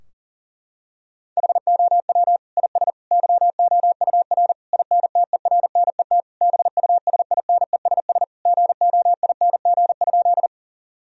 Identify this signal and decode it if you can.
Morse code, heard as a keyed tone, about 37 wpm, 700 Hz. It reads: HOW IS YOUR INTERNET BUSINESS GOING?
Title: Sentences from Top 1000 Words - Review Part 4 - 40wpm